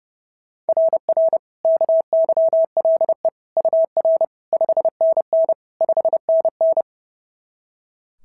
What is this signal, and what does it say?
Morse code, heard as a keyed tone, about 30 wpm, 690 Hz. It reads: RR KYLE UR 5NN 5NN
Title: SOTA QSO Practice 30wpm